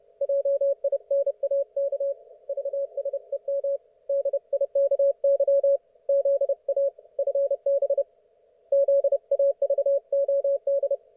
Morse code, heard as a keyed tone, about 30 wpm, 560 Hz. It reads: JINAK VSEM DIKY ZA FB ZAVOD
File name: Archiv / CUC179 / ok1fhi cul179